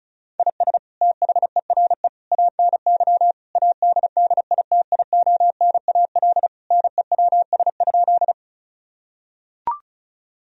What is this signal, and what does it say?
Morse code, heard as a keyed tone, about 34 wpm, 700 Hz. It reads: IS THERE ANY ADDITIONAL NEWS?
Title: Sentences from Top 1000 Words - Review Part 4 - 35wpm